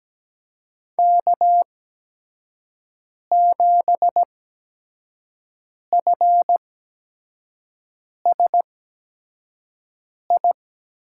Morse code, heard as a keyed tone, about 17 wpm, spaced as for 5 wpm, 700 Hz. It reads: K7FSI
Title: USA Call Signs - 5 Characters - 8x Character Spacing - 17wpm